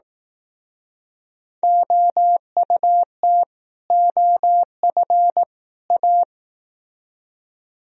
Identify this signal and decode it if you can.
Morse code, heard as a keyed tone, about 19 wpm, 700 Hz. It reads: OUT OF A
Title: Speed-Racing - Sets of 3 Words 15wpm